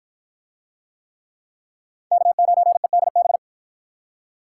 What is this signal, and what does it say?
Morse code, heard as a keyed tone, about 51 wpm, 700 Hz. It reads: K9EDB